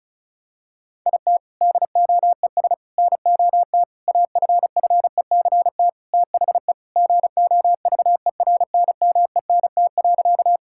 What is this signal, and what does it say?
Morse code, heard as a keyed tone, about 36 wpm, 700 Hz. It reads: IT DOES NOT AFFECT THE GOVERNMENT.
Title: Sentences from Top 1000 Words - Review Part 6 - 35wpm